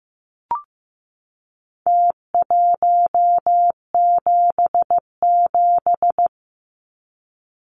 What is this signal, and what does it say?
Morse code, heard as a keyed tone, about 15 wpm, 700 Hz. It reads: T177
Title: Numbers - Contest Style - 4-Digit 15wpm